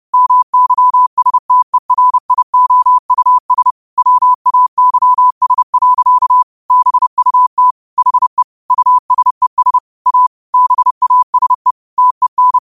Morse code, heard as a keyed tone, about 30 wpm, 1000 Hz. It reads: MYSTERIOUS WAYS. BUT HE USES A BASE TEN
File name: qotdfiles / 30wpm